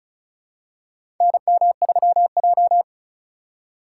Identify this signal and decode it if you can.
Morse code, heard as a keyed tone, about 36 wpm, 700 Hz. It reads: NM3J
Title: USA Call Signs - 4 Characters - 3x Character Spacing - 35wpm